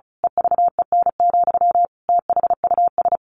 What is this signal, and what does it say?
Morse code, heard as a keyed tone, about 34 wpm, 700 Hz. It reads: EVEN, THUS
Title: A Sound Of Thunder 35wpm